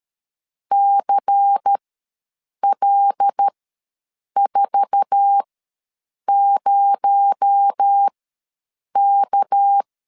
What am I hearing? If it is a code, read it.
Morse code, heard as a keyed tone, about 13 wpm, spaced as for 7 wpm, 800 Hz. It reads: CL40K
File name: LEZIONI / 9 28 ws 12wpm 3354